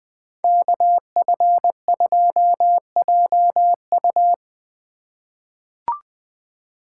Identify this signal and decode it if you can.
Morse code, heard as a keyed tone, about 21 wpm, 700 Hz. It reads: KF2JU
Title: USA Call Signs - 5 Characters - 2x Character Spacing - 20wpm